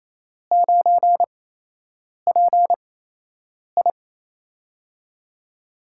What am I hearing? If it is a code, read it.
Morse code, heard as a keyed tone, about 28 wpm, 700 Hz. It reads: 9 P I
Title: USA Call Signs - 4 Characters - 8x Character Spacing - 28wpm